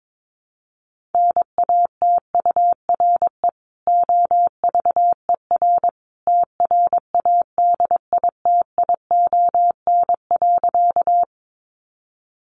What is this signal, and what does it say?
Morse code, heard as a keyed tone, about 22 wpm, 700 Hz. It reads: NATURE OVER TRADITION.
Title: Sentences from Top 1500 Words 22wpm